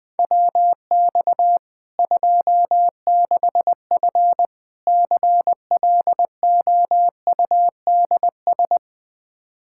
Morse code, heard as a keyed tone, about 20 wpm, 700 Hz. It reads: WX 26F CLOUDS
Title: Practice QSO - Exchange 2 - Info - 20wpm